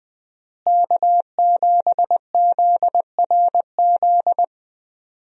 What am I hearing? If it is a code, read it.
Morse code, heard as a keyed tone, about 20 wpm, 700 Hz. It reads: K7ZRZ
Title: USA Call Signs - 5 Characters - Repeated 2x - 20wpm